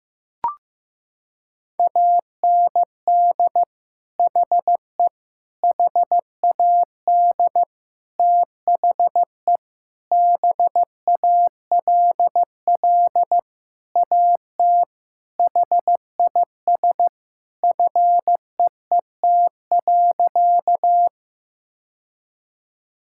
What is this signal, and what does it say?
Morse code, heard as a keyed tone, about 16 wpm, 700 Hz. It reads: AND HE HAD THE BALL AT HIS FEET.
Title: Sentences from Top 1000 Words - Review Part 6 - 15wpm